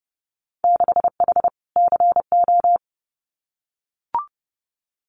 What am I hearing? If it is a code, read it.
Morse code, heard as a keyed tone, about 30 wpm, 700 Hz. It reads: 6H CO
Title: Winter Field Day Exchanges Encore 30wpm